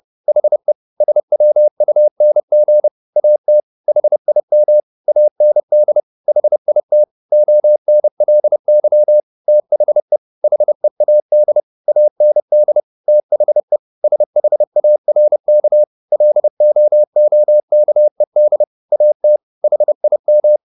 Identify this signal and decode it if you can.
Morse code, heard as a keyed tone, about 30 wpm, 600 Hz. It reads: HE SWUNG AT HIM AND HIT ONLY THE HEAD AND THE SHARK LOOKED AT HIM